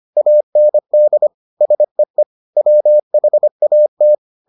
Morse code, heard as a keyed tone, about 25 wpm, 600 Hz. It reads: AND SEE WHAT